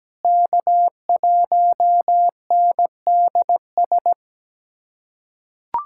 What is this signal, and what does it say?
Morse code, heard as a keyed tone, about 17 wpm, 700 Hz. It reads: K1NDS
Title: USA Callsign DE Callsign 17wpm